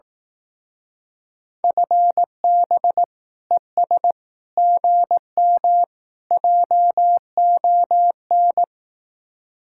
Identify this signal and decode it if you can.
Morse code, heard as a keyed tone, about 18 wpm, 700 Hz. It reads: FB ES GM JON
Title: Speed-Racing Common QSO Phrases Encore - 15wpm